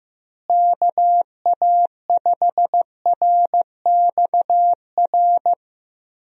Morse code, heard as a keyed tone, about 15 wpm, 700 Hz. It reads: KA5RXR
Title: USA Call Signs - 6 Characters - Repeated 3x - 15wpm